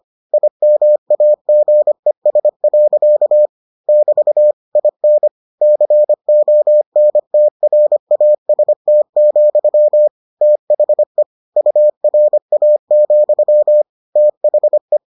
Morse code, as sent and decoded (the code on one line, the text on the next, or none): .. -- .- --. ... .-.-.- / -...- / .. -. / -.-. --- -. - .-. .- ... - --..-- / - .... / ..- .-. .- --..-- / - .... .
IMAGES. <BT> IN CONTRAST, THE URA, THE